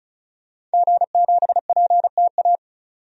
Morse code, heard as a keyed tone, about 34 wpm, 700 Hz. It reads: G7PTA
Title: DXCC-100 Call Signs - 5 Characters - 35wpm